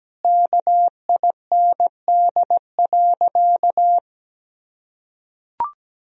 Morse code, heard as a keyed tone, about 17 wpm, 700 Hz. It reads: KIND.
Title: Sentences from Top 200 Words and Abbreviations 17wpm